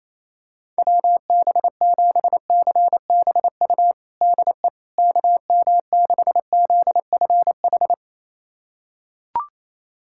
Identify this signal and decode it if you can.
Morse code, heard as a keyed tone, about 28 wpm, 700 Hz. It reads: WB7CBU DE KM6ZFH
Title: USA Callsign DE Callsign - 6 Characters - 28wpm